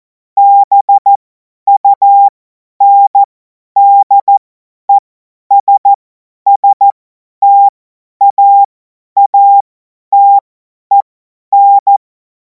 Morse code, heard as a keyed tone, about 14 wpm, spaced as for 10 wpm, 800 Hz. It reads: BUNDESSTAATEN
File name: Morsekurs / Witze / Witz 12. Mai 2026 50 F70